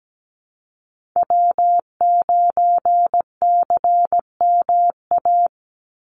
Morse code, heard as a keyed tone, about 17 wpm, 700 Hz. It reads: W9CMA
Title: USA Call Signs 17wpm